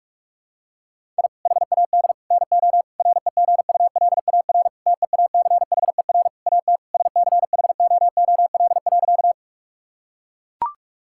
Morse code, heard as a keyed tone, about 41 wpm, 700 Hz. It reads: I HAD NO REGULAR TEACHER AT SCHOOL.
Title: Sentences from Top 1000 Words - Review Part 6 - 45wpm